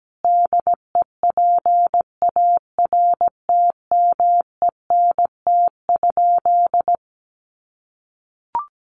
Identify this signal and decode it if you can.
Morse code, heard as a keyed tone, about 17 wpm, 700 Hz. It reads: DEPARTMENT?
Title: Sentences from Top 1200 Words 17wpm